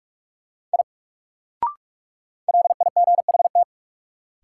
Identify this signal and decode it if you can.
Morse code, heard as a keyed tone, about 42 wpm, 700 Hz. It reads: I LIGHT
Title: Sentences from Top 200 Words - New Word Drill - 45wpm